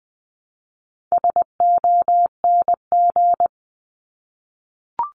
Morse code, heard as a keyed tone, about 20 wpm, 700 Hz. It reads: SONG
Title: Top 500 Words Repeated 20wpm